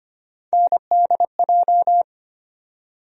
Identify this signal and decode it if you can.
Morse code, heard as a keyed tone, about 25 wpm, 700 Hz. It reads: NDJ
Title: USA Call Sign Postfixes - 3 Letters 25wpm